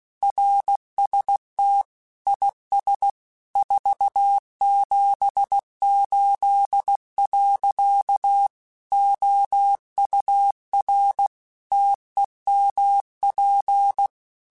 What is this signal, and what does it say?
Morse code, heard as a keyed tone, about 16 wpm, 800 Hz. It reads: RST IS 478. OUR TEMP
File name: QSOs / QSOs-mp3s / WD0BGP 2@15